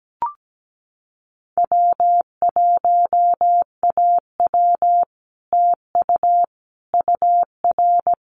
Morse code, heard as a keyed tone, about 17 wpm, 700 Hz. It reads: W1AW TU UR
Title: Common QSO Phrases Encore 17wpm